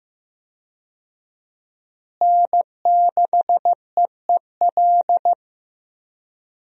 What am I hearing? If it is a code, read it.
Morse code, heard as a keyed tone, about 15 wpm, 700 Hz. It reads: N6EEL
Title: USA Call Signs - 5 Characters - Repeated 2x - 15wpm